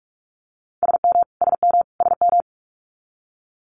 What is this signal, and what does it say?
Morse code, heard as a keyed tone, about 46 wpm, 700 Hz. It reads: SM SM SM
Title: Multi-letter Phonemes Repeated 3x 45wpm